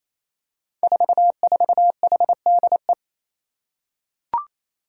Morse code, heard as a keyed tone, about 28 wpm, 700 Hz. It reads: 44HDE
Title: Five Challenging Letter-Number Combinations 28wpm